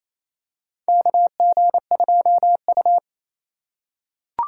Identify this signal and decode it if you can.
Morse code, heard as a keyed tone, about 28 wpm, 700 Hz. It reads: KG2U
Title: USA Call Signs - 4 Characters - 2x Character Spacing - 28wpm